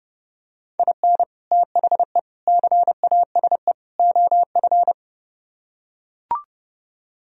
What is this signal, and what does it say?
Morse code, heard as a keyed tone, about 30 wpm, 700 Hz. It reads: IN THE CASE OF